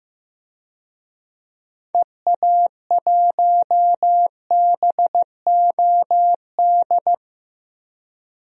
Morse code, heard as a keyed tone, about 15 wpm, 700 Hz. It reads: EA1BOD
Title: DXCC-100 Call Signs - 6 Characters - Repeated 2x - 15wpm